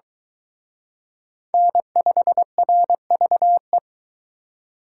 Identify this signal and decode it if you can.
Morse code, heard as a keyed tone, about 22 wpm, 700 Hz. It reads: N5RVE